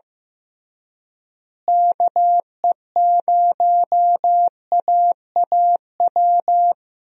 Morse code, heard as a keyed tone, about 15 wpm, 700 Hz. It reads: KE0AAW
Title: USA Call Signs - 6 Characters - Repeated 2x - 15wpm